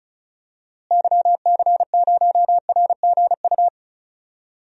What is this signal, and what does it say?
Morse code, heard as a keyed tone, about 36 wpm, 700 Hz. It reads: YC0RGU
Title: DXCC-100 Call Signs - 6 Characters - Repeated 2x - 35wpm